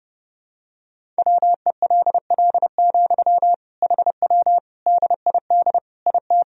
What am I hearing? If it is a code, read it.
Morse code, heard as a keyed tone, about 30 wpm, 700 Hz. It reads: WELL, HW DID IT
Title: Sentences from Top 200 Words and Abbreviations 30wpm